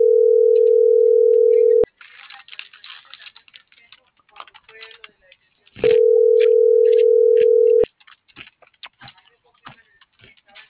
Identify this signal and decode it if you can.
Ringback tone (North America), 440 + 480 Hz.